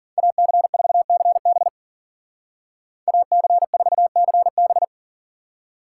Morse code, heard as a keyed tone, about 51 wpm, 700 Hz. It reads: AC4CB AC4CB